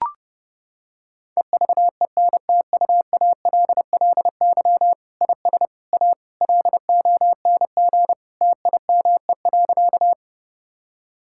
Morse code, heard as a keyed tone, about 30 wpm, 700 Hz. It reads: EVENTUALLY IS A LONG TIME.
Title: Sentences from Top 1000 Words - Review Part 6 - 30wpm